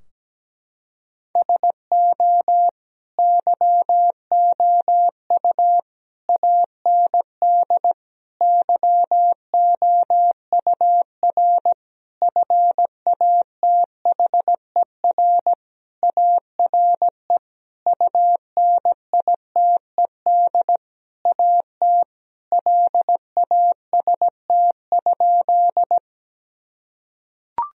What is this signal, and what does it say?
Morse code, heard as a keyed tone, about 17 wpm, 700 Hz. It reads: SO YOU AND YOUR FATHER ARE UNITED AT LAST?
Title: Sentences from Top 1000 Words - Review Part 4 - 17wpm